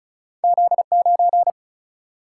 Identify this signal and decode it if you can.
Morse code, heard as a keyed tone, about 36 wpm, 700 Hz. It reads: Z 9
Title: Single-Letter Single-Number 35wpm